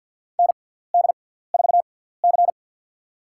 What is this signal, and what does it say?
Morse code, heard as a keyed tone, about 53 wpm, spaced as for 17 wpm, 700 Hz. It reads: ND4C
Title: USA Call Signs - 4 Characters - 6x Character Spacing - 50wpm